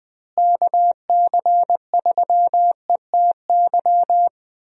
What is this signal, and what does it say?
Morse code, heard as a keyed tone, about 21 wpm, 700 Hz. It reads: KC3ETY
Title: USA Call Signs - 6 Characters - Repeated 2x - 20wpm